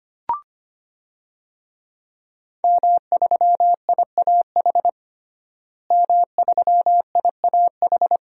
Morse code, heard as a keyed tone, about 25 wpm, 700 Hz. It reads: M3IAH M3IAH
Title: DXCC-100 Call Signs - 5 Characters - Repeated 2x - 25wpm